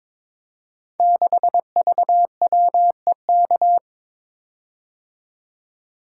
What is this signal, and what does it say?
Morse code, heard as a keyed tone, about 22 wpm, 700 Hz. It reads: 6VWEK
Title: Code Groups - 5 Letter:Numbers - 22wpm